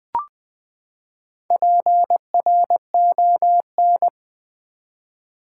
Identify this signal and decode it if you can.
Morse code, heard as a keyed tone, about 20 wpm, 700 Hz. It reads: PRON
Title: Build A Word - 4 Letters At A Time - 10-Letter Words - 20wpm